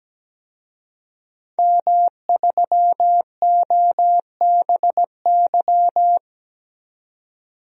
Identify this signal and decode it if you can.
Morse code, heard as a keyed tone, about 17 wpm, 700 Hz. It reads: M3OBY
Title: DXCC-100 Call Signs - 5 Characters - 17wpm